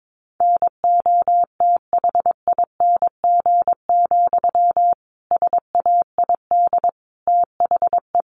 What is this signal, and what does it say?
Morse code, heard as a keyed tone, about 22 wpm, 700 Hz. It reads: NOTHING, SAID THE
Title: Sentences from Top 1300 Words 22wpm